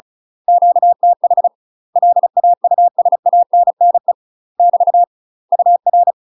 Morse code, heard as a keyed tone, about 36 wpm, 700 Hz. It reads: QTH LAUSANNE <BT> UR
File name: UK MASTER QSOs TRAINING/35WPM